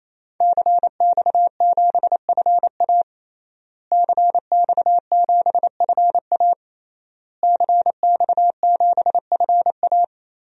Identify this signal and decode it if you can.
Morse code, heard as a keyed tone, about 28 wpm, 700 Hz. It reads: CX7FA CX7FA CX7FA
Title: DXCC-100 Call Signs - 5 Characters - Repeated 3x - 28wpm